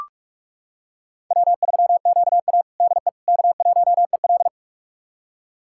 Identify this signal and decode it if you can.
Morse code, heard as a keyed tone, about 45 wpm, 700 Hz. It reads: W3QA DE K1EL
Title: USA Callsign DE Callsign - 4 Characters - 45wpm